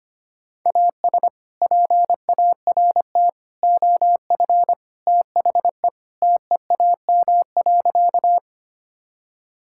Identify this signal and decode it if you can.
Morse code, heard as a keyed tone, about 25 wpm, 700 Hz. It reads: AS PART OF THE TEAM.
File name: practice / 25wpm / Sentences with 3-Word Phrases - 25wpm